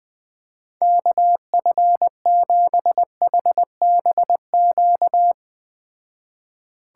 Morse code, heard as a keyed tone, about 20 wpm, 700 Hz. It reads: KF7HBQ